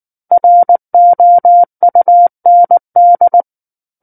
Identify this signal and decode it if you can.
Morse code, heard as a keyed tone, about 19 wpm, 700 Hz. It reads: ROUND
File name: Book One / 10-Chapter 10 - 19 WPM